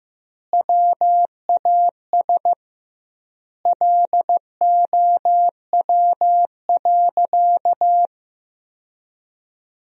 Morse code, heard as a keyed tone, about 15 wpm, 700 Hz. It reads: WAS LOW.
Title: 2x Word-Spacing - Sentences from Top 500 Words - 15wpm